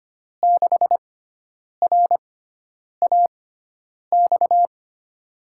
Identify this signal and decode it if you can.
Morse code, heard as a keyed tone, about 25 wpm, spaced as for 9 wpm, 700 Hz. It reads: 6RAX
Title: USA Call Signs - 6 Characters - 6x Character Spacing - 25wpm